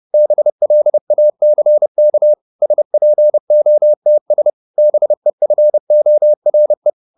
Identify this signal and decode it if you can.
Morse code, heard as a keyed tone, about 30 wpm, 600 Hz. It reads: BLACK SPOTS BEFORE